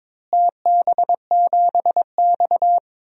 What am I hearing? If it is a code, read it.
Morse code, heard as a keyed tone, about 22 wpm, 700 Hz. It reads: TB7X
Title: DXCC-100 Call Signs - 4 Characters - Repeated 2x - 22wpm